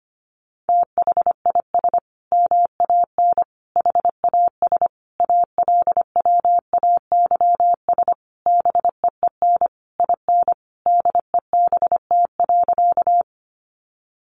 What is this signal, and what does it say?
Morse code, heard as a keyed tone, about 25 wpm, 700 Hz. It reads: THIS MAN HAS ALWAYS BEEN IN DEBT.
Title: Sentences from Top 1300 Words 25wpm